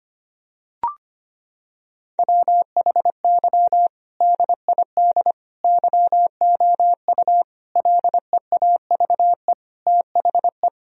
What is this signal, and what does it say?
Morse code, heard as a keyed tone, about 25 wpm, 700 Hz. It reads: WHY DID YOU LEAVE THE
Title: Sentences from Top 1000 Words - Review Part 3 - 25wpm